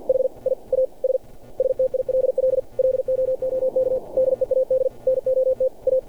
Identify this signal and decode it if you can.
Morse code, heard as a keyed tone, about 53 wpm, 520 Hz. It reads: HI AS STILL LOOK DEAD NOT A
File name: archived / CRSnet